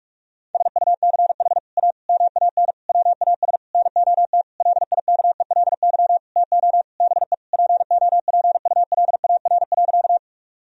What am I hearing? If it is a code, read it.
Morse code, heard as a keyed tone, about 41 wpm, 700 Hz. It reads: SUCH A MAN WAS NOT LIKELY TO BE POPULAR.
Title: Sentences from Top 1000 Words - Review Part 4 - 45wpm